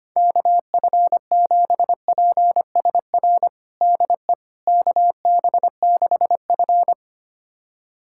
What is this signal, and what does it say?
Morse code, heard as a keyed tone, about 25 wpm, 700 Hz. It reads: KF7PSR DE KB6F
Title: USA Callsign DE Callsign 25wpm